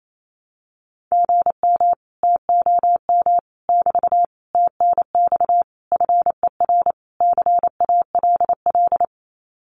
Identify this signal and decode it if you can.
Morse code, heard as a keyed tone, about 28 wpm, 700 Hz. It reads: GM TOM <BT> TNX FER CALL
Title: Common QSO Phrases Encore 28wpm